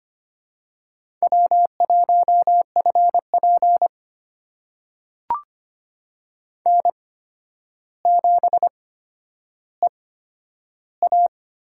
Morse code, heard as a keyed tone, about 25 wpm, spaced as for 11 wpm, 700 Hz. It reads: W1FP N7EA
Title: USA Call Signs - 4 Characters - 8x Character Spacing - 25wpm